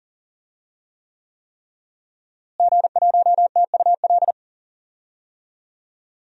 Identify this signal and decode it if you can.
Morse code, heard as a keyed tone, about 40 wpm, 700 Hz. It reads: G1TUL
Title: DXCC-100 Call Signs - 5 Characters - Repeated 3x - 40wpm